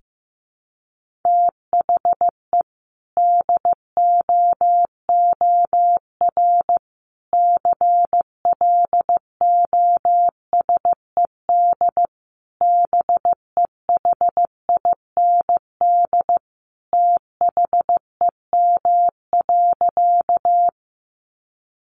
Morse code, heard as a keyed tone, about 15 wpm, 700 Hz. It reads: THE DOOR CLOSED BEHIND THEM.
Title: Sentences from Top 1400 Words 15wpm